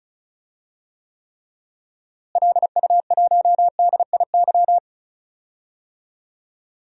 Morse code, heard as a keyed tone, about 34 wpm, 700 Hz. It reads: LU1DIY
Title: DXCC-100 Call Signs - 6 Characters - Repeated 3x - 35wpm